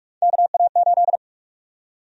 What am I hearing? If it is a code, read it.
Morse code, heard as a keyed tone, about 46 wpm, 700 Hz. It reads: KA8
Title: USA Call Sign Prefixes - Repeated 2x - 45wpm